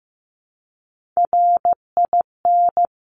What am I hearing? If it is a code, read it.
Morse code, heard as a keyed tone, about 15 wpm, 700 Hz. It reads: RIN
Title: Top Three Letter Combinations in Words 15wpm